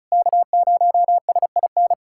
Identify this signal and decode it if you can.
Morse code, heard as a keyed tone, about 36 wpm, 700 Hz. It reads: K0SIN
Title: USA Callsign DE Callsign - 5 Characters - 35wpm